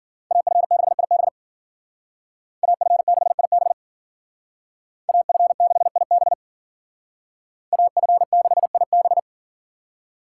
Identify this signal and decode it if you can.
Morse code, heard as a keyed tone, about 52 wpm, 700 Hz. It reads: AF6IB AF6IB AF6IB AF6IB